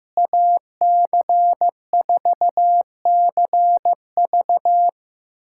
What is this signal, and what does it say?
Morse code, heard as a keyed tone, about 15 wpm, 700 Hz. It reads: AC4CV
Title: USA Callsign DE Callsign 15wpm